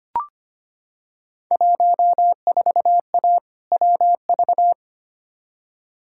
Morse code, heard as a keyed tone, about 25 wpm, 700 Hz. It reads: 14A WV
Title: Field Day Exchanges Encore - 25wpm